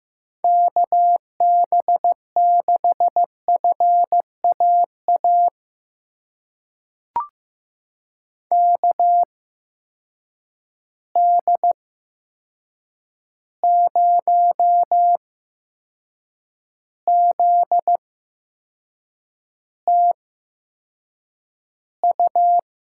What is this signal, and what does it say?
Morse code, heard as a keyed tone, about 15 wpm, spaced as for 6 wpm, 700 Hz. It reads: KB6FAA KD0ZTU
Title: USA Call Signs - 6 Characters - 8x Character Spacing - 15wpm